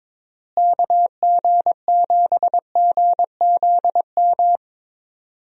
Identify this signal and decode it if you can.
Morse code, heard as a keyed tone, about 22 wpm, 700 Hz. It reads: KG7GZM
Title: USA Call Signs - 6 Characters - Repeated 2x - 22wpm